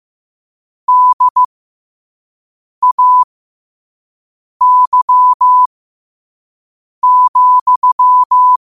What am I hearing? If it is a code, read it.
Morse code, heard as a keyed tone, about 15 wpm, spaced as for 5 wpm, 1000 Hz. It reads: DAY,
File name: qotdfiles / 5wpm